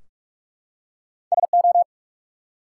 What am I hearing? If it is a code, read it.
Morse code, heard as a keyed tone, about 44 wpm, 700 Hz. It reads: SO
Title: Two Letter Words - 45wpm